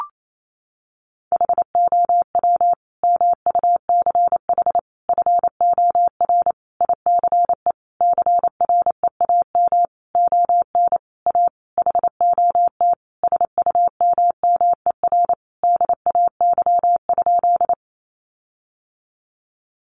Morse code, heard as a keyed tone, about 28 wpm, 700 Hz. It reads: HOW MUCH FOR ICE CREAM ON A HOT SUMMER DAY?
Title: Sentences from Top 1500 Words 28wpm